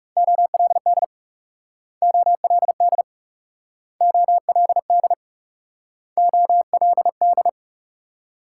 Morse code, heard as a keyed tone, about 39 wpm, 700 Hz. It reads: OLD OLD OLD OLD